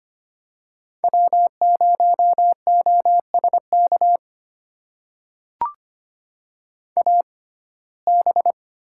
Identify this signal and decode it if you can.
Morse code, heard as a keyed tone, about 25 wpm, spaced as for 18 wpm, 700 Hz. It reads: W0OSK AB
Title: USA Call Signs - 5 Characters - 6x Character Spacing - 25wpm